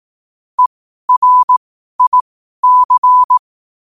Morse code, heard as a keyed tone, about 18 wpm, spaced as for 12 wpm, 1000 Hz. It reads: ERIC
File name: qotdfiles / 12wpm